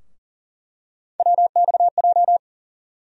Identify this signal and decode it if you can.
Morse code, heard as a keyed tone, about 40 wpm, 700 Hz. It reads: WXJ
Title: USA Call Sign Postfixes - 3 Letters 40wpm